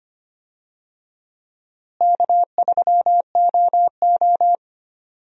Morse code, heard as a keyed tone, about 25 wpm, 700 Hz. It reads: K3OO
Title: USA Call Signs - 4 Characters - Repeated 3x - 25wpm